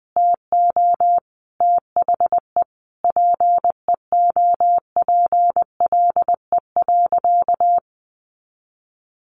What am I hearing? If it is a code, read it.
Morse code, heard as a keyed tone, about 20 wpm, 700 Hz. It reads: TO THE PEOPLE.
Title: Sentences from Top 400 Words 20wpm